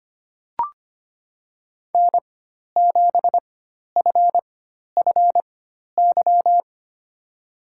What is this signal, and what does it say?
Morse code, heard as a keyed tone, about 25 wpm, spaced as for 12 wpm, 700 Hz. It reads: N7FFY
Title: USA Call Signs - 5 Characters - 4x Character Spacing - 25wpm